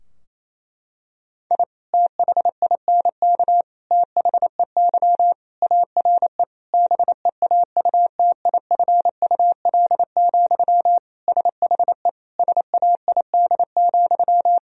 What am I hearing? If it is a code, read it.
Morse code, heard as a keyed tone, about 27 wpm, 700 Hz. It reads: I THINK THEY ARE BEAUTIFUL, SHE SAID,
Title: Sentences from Top 1000 Words - Review Part 4 - 28wpm